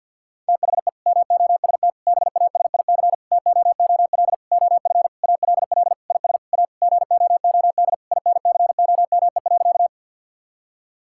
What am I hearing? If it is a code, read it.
Morse code, heard as a keyed tone, about 50 wpm, 700 Hz. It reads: THE MOST BASIC TOOL OF ALL IS A GOOD INCOME.
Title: Sentences from Top 1000 Words - Review Part 6 - 50wpm